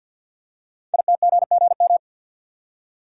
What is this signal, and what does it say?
Morse code, heard as a keyed tone, about 50 wpm, 700 Hz. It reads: ITGGM
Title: Code Groups - 5 Letters - 50wpm